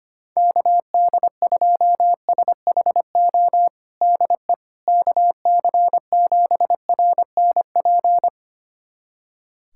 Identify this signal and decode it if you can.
Morse code, heard as a keyed tone, about 25 wpm, 700 Hz. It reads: KD2SHO DE KC7RNP
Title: USA Callsign DE Callsign - 6 Characters - 25wpm